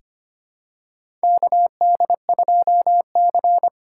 Morse code, heard as a keyed tone, about 25 wpm, 700 Hz. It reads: KD2C
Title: USA Call Signs - 2x Character Spacing - 25wpm